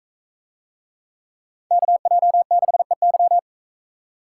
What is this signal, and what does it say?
Morse code, heard as a keyed tone, about 44 wpm, 700 Hz. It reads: KJ6EY